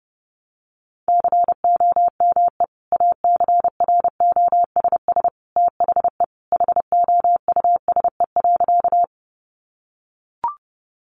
Morse code, heard as a keyed tone, about 30 wpm, 700 Hz. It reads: COME ACROSS THE HOUSE.
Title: Sentences from Top 700 Words 30wpm